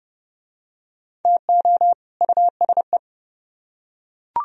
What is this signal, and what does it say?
Morse code, heard as a keyed tone, about 29 wpm, 700 Hz. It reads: TO USE